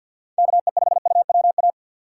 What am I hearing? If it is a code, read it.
Morse code, heard as a keyed tone, about 53 wpm, 700 Hz. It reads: KE5UWA
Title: USA Call Signs - 6 Characters - 3x Character Spacing - 50wpm